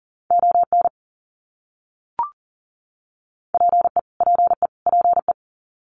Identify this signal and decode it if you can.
Morse code, heard as a keyed tone, about 40 wpm, 700 Hz. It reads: ON PE PE PE
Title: Canadian Provinces and Territories - Repeated 3x - 40wpm